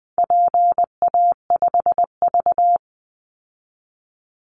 Morse code, heard as a keyed tone, about 19 wpm, 700 Hz. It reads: PA5V
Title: Call Signs Repeated 3x - 20wpm